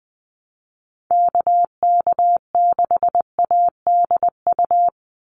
Morse code, heard as a keyed tone, about 20 wpm, 700 Hz. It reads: KK6ADU
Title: USA Call Signs 20wpm